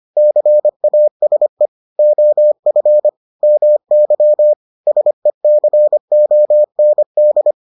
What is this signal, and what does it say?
Morse code, heard as a keyed tone, about 25 wpm, 600 Hz. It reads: CASE OF MY SECOND